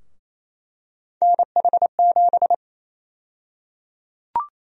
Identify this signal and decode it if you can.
Morse code, heard as a keyed tone, about 28 wpm, 700 Hz. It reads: NH7
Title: USA Call Sign Prefixes - Repeated 2x - 28wpm